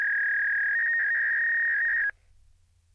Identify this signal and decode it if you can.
Fax machine handshake.
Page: fax-dialling-fax-machine